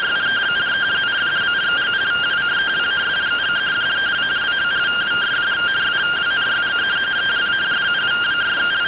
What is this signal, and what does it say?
Two-tone teleprinter signal, about 170 Hz shift.
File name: findings / CT 205 ax mips, iDose 3 0004.gen-27 rtty rx